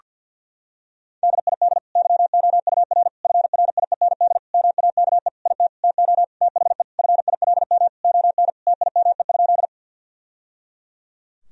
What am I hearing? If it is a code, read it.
Morse code, heard as a keyed tone, about 47 wpm, 700 Hz. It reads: DID YOUR FRIEND MAKE IT TO THE FILM ON TIME?
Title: Sentences from Top 1000 Words - Review Part 3 - 50wpm